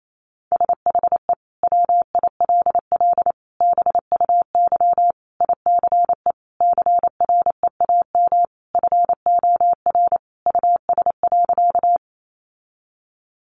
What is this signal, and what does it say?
Morse code, heard as a keyed tone, about 28 wpm, 700 Hz. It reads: SHE WILL BUY ICE CREAM FOR US.
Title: Sentences from Top 1500 Words 28wpm